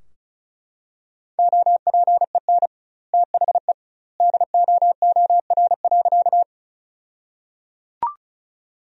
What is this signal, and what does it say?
Morse code, heard as a keyed tone, about 36 wpm, 700 Hz. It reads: OPEN THE DOOR.
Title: 2x Word-Spacing - Sentences from Top 300 Words - 35wpm